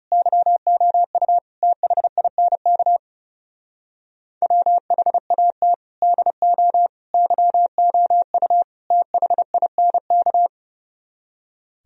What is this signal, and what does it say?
Morse code, heard as a keyed tone, about 37 wpm, 700 Hz. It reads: YOU THINK WHAT DO YOU THINK
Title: Speed-Racing - Sets of 4 Words 30wpm